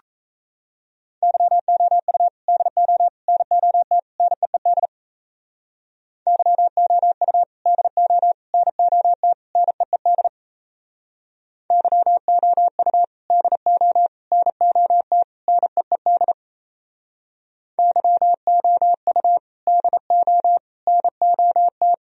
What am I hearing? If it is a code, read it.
Morse code, heard as a keyed tone, about 34 wpm, 700 Hz. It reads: YOU DO NOT NEED YOU DO NOT NEED YOU DO NOT NEED YOU DO NOT
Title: Speed-Racing - Sets of 4 Words 28wpm